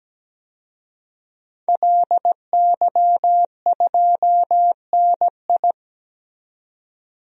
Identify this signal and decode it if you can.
Morse code, heard as a keyed tone, about 17 wpm, 700 Hz. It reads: LY2NI
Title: DXCC-100 Call Signs - 5 Characters - 17wpm